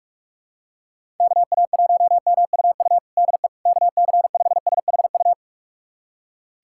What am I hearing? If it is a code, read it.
Morse code, heard as a keyed tone, about 44 wpm, 700 Hz. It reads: KA1MUU DE KC5SHU
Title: USA Callsign DE Callsign - 6 Characters - 45wpm